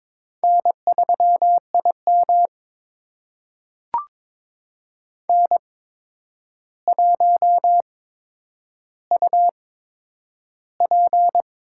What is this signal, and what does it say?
Morse code, heard as a keyed tone, about 22 wpm, spaced as for 9 wpm, 700 Hz. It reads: N3IM N1UP
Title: USA Call Signs - 4 Characters - 8x Character Spacing - 22wpm